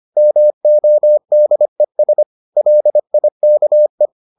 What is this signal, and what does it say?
Morse code, heard as a keyed tone, about 25 wpm, 600 Hz. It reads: MODES LIKE